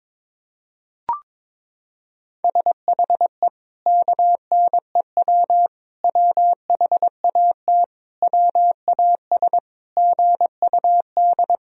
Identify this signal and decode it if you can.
Morse code, heard as a keyed tone, about 22 wpm, 700 Hz. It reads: SHE KNEW WHAT WAS GUD
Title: Sentences from Top 400 Words and Abbreviations 22wpm